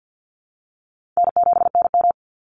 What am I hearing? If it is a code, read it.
Morse code, heard as a keyed tone, about 50 wpm, 700 Hz. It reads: N7NM
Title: Easy Call Signs Repeated 2x - 50wpm